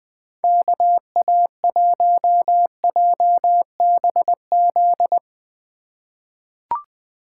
Morse code, heard as a keyed tone, about 20 wpm, 700 Hz. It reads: KA1JBZ
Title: USA Call Signs - 6 Characters - 4x Character Spacing - 20wpm